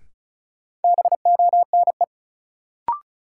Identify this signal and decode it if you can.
Morse code, heard as a keyed tone, about 36 wpm, 700 Hz. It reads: BONE
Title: Sentences from Top 1600 Words - New Word Drill - 35wpm